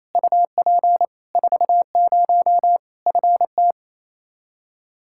Morse code, heard as a keyed tone, about 27 wpm, 700 Hz. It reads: UP 40 FT
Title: Practice QSO - Exchange 2 - Info - 28wpm